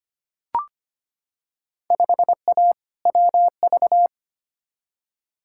Morse code, heard as a keyed tone, about 25 wpm, 700 Hz. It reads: 5A WV
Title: Field Day Exchanges - 25wpm